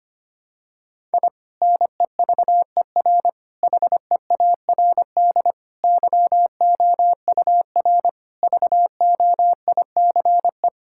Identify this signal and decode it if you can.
Morse code, heard as a keyed tone, about 25 wpm, 700 Hz. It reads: I NEVER HEARD YOUR VOICE
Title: Sentences from Top 1000 Words - Review Part 6 - 25wpm